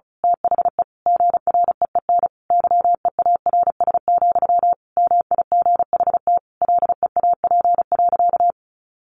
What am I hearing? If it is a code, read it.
Morse code, heard as a keyed tone, about 34 wpm, 700 Hz. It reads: THE GREEN YEARS, MIGHT LEAP.
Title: A Sound Of Thunder 35wpm